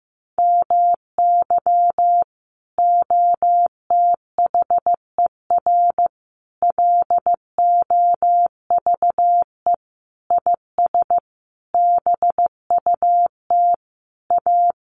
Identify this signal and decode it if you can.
Morse code, heard as a keyed tone, about 15 wpm, 700 Hz. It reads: MY OTHER LOVE IS BUT A
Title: Sentences from Top 1300 Words 15wpm